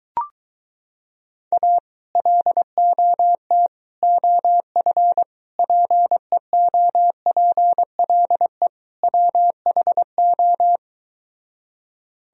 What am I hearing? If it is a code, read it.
Morse code, heard as a keyed tone, about 23 wpm, 700 Hz. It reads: A LOT OF PEOPLE WHO